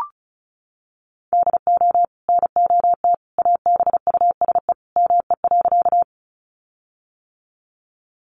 Morse code, heard as a keyed tone, about 34 wpm, 700 Hz. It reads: DO NOT ABUSE ME.
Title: Sentences from Top 1300 Words 35wpm